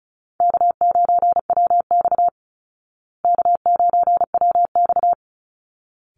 Morse code, heard as a keyed tone, about 34 wpm, 700 Hz. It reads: K9WX K9WX
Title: Call Signs Repeated 3x - 35wpm